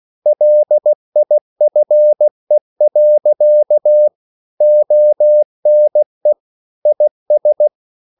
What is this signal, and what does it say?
Morse code, heard as a keyed tone, about 16 wpm, 600 Hz. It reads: LIFE. ONE IS
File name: CW / Practice / EnglishQuotations0416